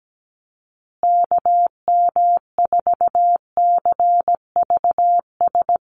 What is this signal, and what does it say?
Morse code, heard as a keyed tone, about 17 wpm, 700 Hz. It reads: KM4CVS
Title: USA Call Signs Repeated 3x 17wpm